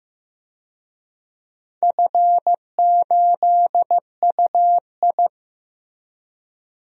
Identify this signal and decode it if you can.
Morse code, heard as a keyed tone, about 15 wpm, 700 Hz. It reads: F8UI
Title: DXCC-100 Call Signs - 4 Characters - Repeated 2x - 15wpm